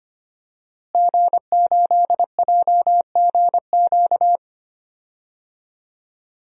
Morse code, heard as a keyed tone, about 25 wpm, 700 Hz. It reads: G8JGQ
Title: DXCC-100 Call Signs - 5 Characters - Repeated 2x - 25wpm